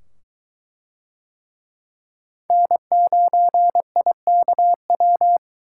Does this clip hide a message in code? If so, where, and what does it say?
Yes: Morse code from 2.5 to 5.4 s, heard as a keyed tone, about 23 wpm, 700 Hz. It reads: N9IKW